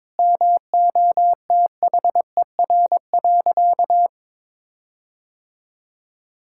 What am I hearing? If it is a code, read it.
Morse code, heard as a keyed tone, about 21 wpm, 700 Hz. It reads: MOTHER.
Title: Sentences from Top 1600 Words 22wpm